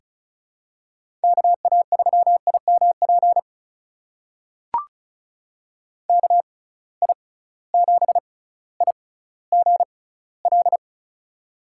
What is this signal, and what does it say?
Morse code, heard as a keyed tone, about 36 wpm, spaced as for 18 wpm, 700 Hz. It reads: KA3IMP KI7IGL
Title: USA Call Signs - 6x Character Spacing - 35wpm